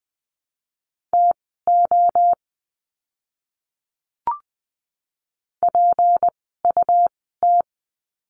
Morse code, heard as a keyed tone, about 20 wpm, spaced as for 15 wpm, 700 Hz. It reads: TO PUT
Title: 237 Lesson 38f - Letter q - Words 20wpm